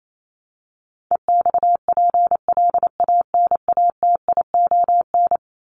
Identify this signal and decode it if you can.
Morse code, heard as a keyed tone, about 28 wpm, 700 Hz. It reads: EXPLANATION
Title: Top 1751-2000 Words 28wpm